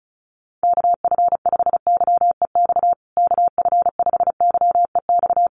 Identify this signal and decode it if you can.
Morse code, heard as a keyed tone, about 34 wpm, 700 Hz. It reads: KF5YEX KF5YEX
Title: USA Call Signs Repeated 3x 35wpm